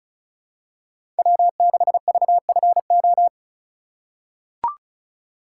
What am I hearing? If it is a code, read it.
Morse code, heard as a keyed tone, about 36 wpm, 700 Hz. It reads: W6VFO
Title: USA Call Signs - 5 Characters - 3x Character Spacing - 35wpm